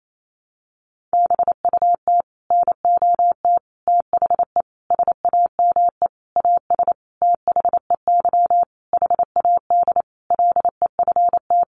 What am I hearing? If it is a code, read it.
Morse code, heard as a keyed tone, about 28 wpm, 700 Hz. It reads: BUT NOT THE SAME AS THEY HAD LEFT
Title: A Sound Of Thunder 28wpm